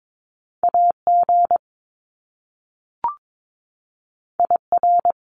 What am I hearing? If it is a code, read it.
Morse code, heard as a keyed tone, about 22 wpm, 700 Hz. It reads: AG IR
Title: Top Two Letter Combinations in Words 22wpm